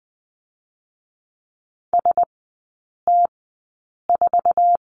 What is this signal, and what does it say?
Morse code, heard as a keyed tone, about 20 wpm, 700 Hz. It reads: S T 4
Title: 172 Lesson 28e - Number 9 - Three Characters 20wpm